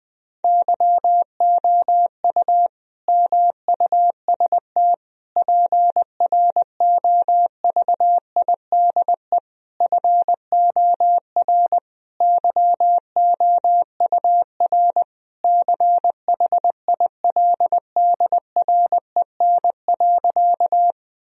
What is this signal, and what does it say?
Morse code, heard as a keyed tone, about 20 wpm, 700 Hz. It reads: YOU MUST PROVIDE FOR YOUR CHILDREN.
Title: Sentences from Top 1000 Words - Review Part 3 - 20wpm